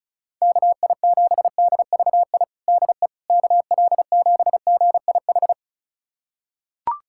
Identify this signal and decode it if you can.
Morse code, heard as a keyed tone, about 34 wpm, 700 Hz. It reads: KI7DVI DE KL7GIH
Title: USA Callsign DE Callsign - 6 Characters - 35wpm